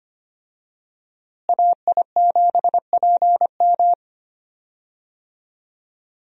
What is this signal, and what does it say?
Morse code, heard as a keyed tone, about 25 wpm, 700 Hz. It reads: AI7PM
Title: USA Call Signs - 5 Characters - Repeated 3x - 25wpm